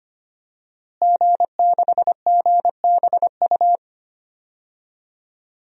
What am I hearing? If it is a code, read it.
Morse code, heard as a keyed tone, about 25 wpm, 700 Hz. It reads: G6GBU
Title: DXCC-100 Call Signs - 5 Characters - Repeated 3x - 25wpm